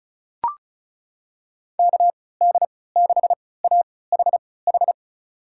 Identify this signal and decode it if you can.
Morse code, heard as a keyed tone, about 34 wpm, spaced as for 20 wpm, 700 Hz. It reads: KD6AHH
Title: USA Call Signs - 6 Characters - 3x Character Spacing - 35wpm